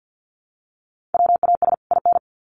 Morse code, heard as a keyed tone, about 50 wpm, 700 Hz. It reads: WAS IN
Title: Sets of 2 Words 50wpm